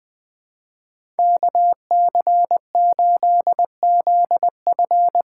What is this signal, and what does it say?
Morse code, heard as a keyed tone, about 20 wpm, 700 Hz. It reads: KC8ZF
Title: USA Call Signs - 6 Characters - 8x Character Spacing - 20wpm